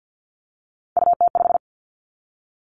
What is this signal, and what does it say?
Morse code, heard as a keyed tone, about 50 wpm, 700 Hz. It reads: UT5
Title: Callsign Prefixes for Top 25 DXCC - 50wpm